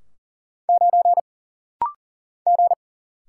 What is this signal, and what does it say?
Morse code, heard as a keyed tone, about 40 wpm, 700 Hz. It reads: 9 G
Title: Single Letter-Number Alt - Rapid-Fire 40wpm